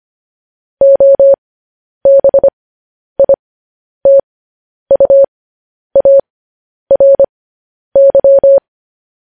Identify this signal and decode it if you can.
Morse code, heard as a keyed tone, about 25 wpm, spaced as for 10 wpm, 550 Hz. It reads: OBITUARY
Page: news headlines in morse code at 10 wpm